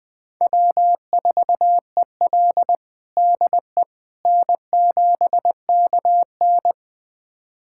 Morse code, heard as a keyed tone, about 20 wpm, 700 Hz. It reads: W4EL DE N7KN
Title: USA Callsign DE Callsign - 4 Characters - 20wpm